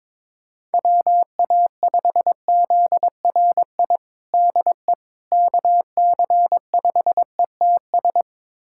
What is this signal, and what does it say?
Morse code, heard as a keyed tone, about 22 wpm, 700 Hz. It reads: WA5ZRI DE KC5ETS
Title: USA Callsign DE Callsign 22wpm